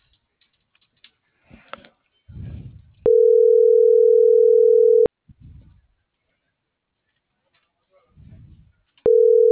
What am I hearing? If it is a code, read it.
Ringback tone (North America), 440 + 480 Hz.